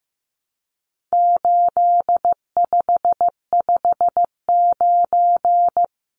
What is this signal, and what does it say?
Morse code, heard as a keyed tone, about 15 wpm, 700 Hz. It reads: 8559
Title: Numbers - 4-Digit 15wpm